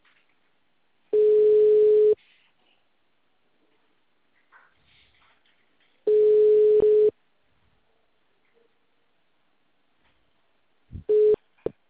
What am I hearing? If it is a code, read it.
Ringback tone (Europe, 425 Hz).